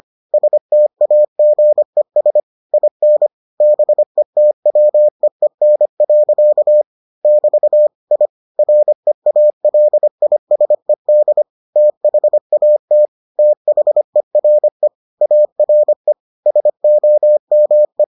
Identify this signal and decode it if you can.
Morse code, heard as a keyed tone, about 25 wpm, 600 Hz. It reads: STAGES IN BETWEEN. <BT> I REALISED THAT THERE ARE SOME